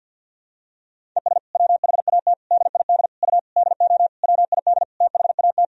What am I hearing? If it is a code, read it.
Morse code, heard as a keyed tone, about 50 wpm, 700 Hz. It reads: ES WHAT DID U DO WID THAT
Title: Sentences from Top 300 Words and Abbreviations 50wpm